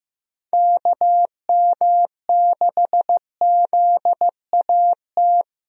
Morse code, heard as a keyed tone, about 15 wpm, 700 Hz. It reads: KM6ZAT
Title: USA Call Signs - 6 Characters - Repeated 2x - 15wpm